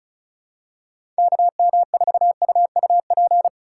Morse code, heard as a keyed tone, about 36 wpm, 700 Hz. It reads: KM4UUP
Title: USA Call Signs - 6 Characters - 3x Character Spacing - 35wpm